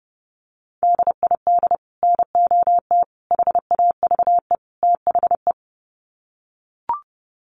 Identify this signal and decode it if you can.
Morse code, heard as a keyed tone, about 30 wpm, 700 Hz. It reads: DID NOT HAVE THE
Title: Sets of 4 Words 30wpm